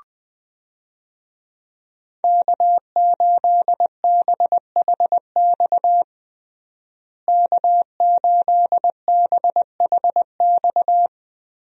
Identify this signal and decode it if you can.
Morse code, heard as a keyed tone, about 20 wpm, 700 Hz. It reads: K8BHX K8BHX
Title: USA Call Signs - 5 Characters - Repeated 3x - 20wpm